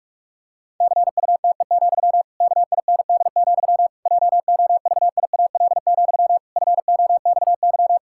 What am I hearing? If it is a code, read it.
Morse code, heard as a keyed tone, about 41 wpm, 700 Hz. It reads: CUTE, KIND, JOVIAL, FOXY
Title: Pangram Challenge v2 - 45wpm